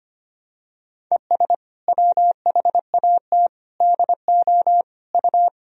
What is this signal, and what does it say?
Morse code, heard as a keyed tone, about 25 wpm, 700 Hz. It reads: ES WHAT DO U
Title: Sentences from Top 100 Words and Abbreviations 25wpm